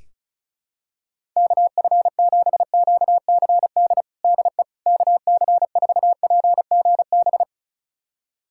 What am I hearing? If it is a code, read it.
Morse code, heard as a keyed tone, about 36 wpm, 700 Hz. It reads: KF7QCD DE KC4PGB
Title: USA Callsign DE Callsign 35wpm